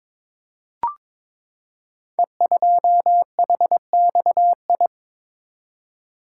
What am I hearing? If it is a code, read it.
Morse code, heard as a keyed tone, about 22 wpm, 700 Hz. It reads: E2HXI
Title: Code Groups - 5 Letter:Numbers - 22wpm